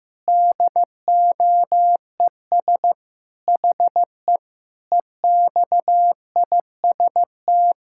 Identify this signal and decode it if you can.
Morse code, heard as a keyed tone, about 16 wpm, 700 Hz. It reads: DOES HE EXIST
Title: Sentences from Top 1000 Words - Review Part 5 - 15wpm